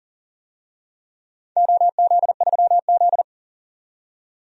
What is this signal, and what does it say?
Morse code, heard as a keyed tone, about 40 wpm, 700 Hz. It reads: OZ3Z
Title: DXCC-100 Call Signs - 4 Characters - 40wpm